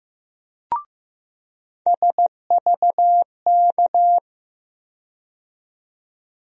Morse code, heard as a keyed tone, about 15 wpm, 700 Hz. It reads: SVK
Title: USA Call Sign Postfixes - 3 Letters 15wpm